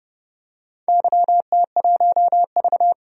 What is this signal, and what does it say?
Morse code, heard as a keyed tone, about 31 wpm, 700 Hz. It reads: YT1V
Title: DXCC-100 Call Signs - 4 Characters - 30wpm